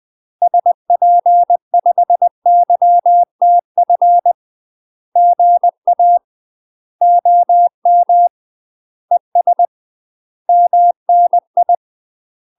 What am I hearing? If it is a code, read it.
Morse code, heard as a keyed tone, about 20 wpm, 700 Hz. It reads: SP5YTF GA OM ES MNI
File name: UK MASTER QSOs TRAINING / 20WPM / UK MASTER QSOs 0069